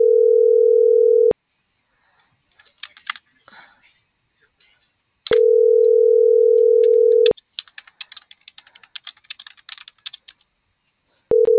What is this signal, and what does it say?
Ringback tone (North America), 440 + 480 Hz.